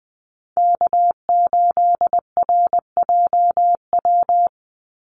Morse code, heard as a keyed tone, about 20 wpm, 700 Hz. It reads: K8RJW
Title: Call Signs - 20wpm